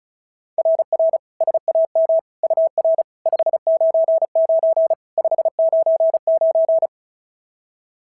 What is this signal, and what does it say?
Morse code, heard as a keyed tone, about 33 wpm, 650 Hz. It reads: RR SAM UR 599 599
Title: SOTA QSO Practice 35wpm